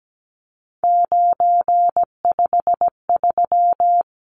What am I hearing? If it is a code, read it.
Morse code, heard as a keyed tone, about 17 wpm, 700 Hz. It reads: 953
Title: Numbers - 3-Digit 17wpm